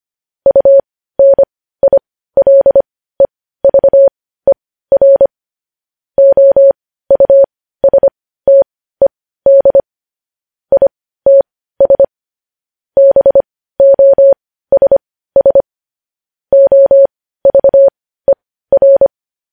Morse code, heard as a keyed tone, about 25 wpm, spaced as for 15 wpm, 550 Hz. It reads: UNILEVER OUSTED ITS BOSS OVER
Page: news-headlines-in-morse-code-at-15-wpm